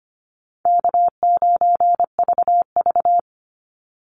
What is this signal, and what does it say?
Morse code, heard as a keyed tone, about 25 wpm, 700 Hz. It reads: K9VV
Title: Easy Call Signs Repeated 3x - 25wpm